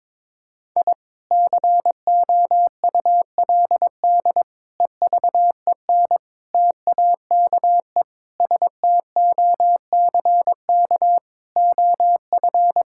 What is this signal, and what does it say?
Morse code, heard as a keyed tone, about 22 wpm, 700 Hz. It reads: I COULD EVEN TAKE STOCK OF
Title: Sentences from Top 1000 Words - Review Part 5 - 22wpm